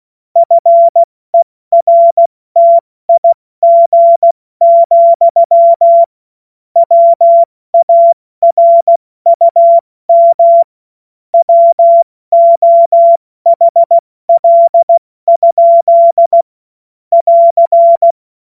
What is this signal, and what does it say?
Morse code, heard as a keyed tone, about 16 wpm, 680 Hz. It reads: FERTIG, WARUM WOHL? <AR>
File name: Morsekurs / Witze / Witz 20. Januar 2026 70 F80